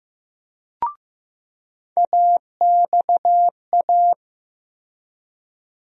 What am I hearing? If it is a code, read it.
Morse code, heard as a keyed tone, about 15 wpm, 700 Hz. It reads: AXA
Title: USA Call Sign Postfixes - 3 Letters 15wpm